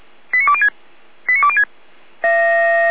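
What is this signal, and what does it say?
Five-tone selective call (ZVEI), heard as quick runs of five tones, tones of about 80 ms. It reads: 89206 89206
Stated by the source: ZVEI five-tone selective call received on SDR.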